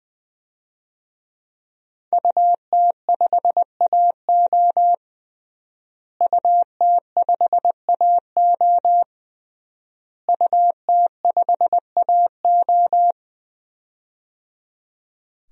Morse code, heard as a keyed tone, about 21 wpm, 700 Hz. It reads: UT5AO UT5AO UT5AO
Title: DXCC-100 Call Signs - 5 Characters - Repeated 3x - 20wpm